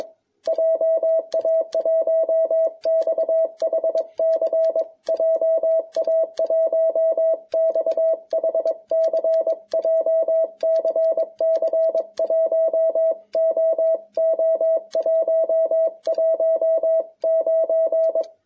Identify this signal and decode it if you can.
Morse code, heard as a keyed tone, about 22 wpm, 660 Hz. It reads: JA1XHC JA1XHC JCC 1OO119